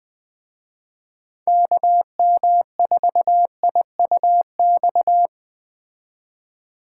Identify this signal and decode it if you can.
Morse code, heard as a keyed tone, about 21 wpm, 700 Hz. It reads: KM4IUX